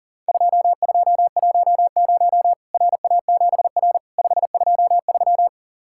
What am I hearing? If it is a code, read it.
Morse code, heard as a keyed tone, about 40 wpm, 700 Hz. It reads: 2210 RA7R 523
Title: Worked All Europe DX Contest - QTC Exchange - 40wpm